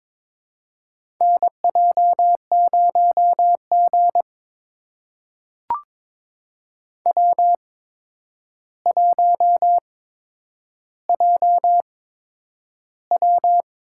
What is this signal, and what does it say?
Morse code, heard as a keyed tone, about 22 wpm, spaced as for 9 wpm, 700 Hz. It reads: NJ0G W1JW
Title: USA Call Signs - 4 Characters - 8x Character Spacing - 22wpm